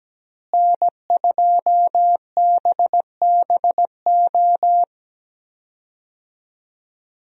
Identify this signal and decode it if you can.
Morse code, heard as a keyed tone, about 17 wpm, 700 Hz. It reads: N2BBO
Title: USA Call Signs - 5 Characters - Repeated 2x - 17wpm